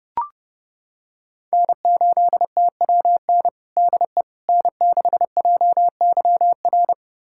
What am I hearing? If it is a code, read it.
Morse code, heard as a keyed tone, about 30 wpm, 700 Hz. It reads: N8TWN DE N6JYR
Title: USA Callsign DE Callsign 30wpm